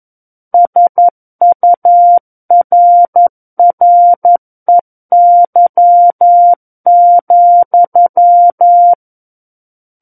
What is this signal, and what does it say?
Morse code, heard as a keyed tone, about 11 wpm, 700 Hz. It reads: SURREY,
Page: SKCC - War of the Worlds - 02-Chapter 2 - 11 WPM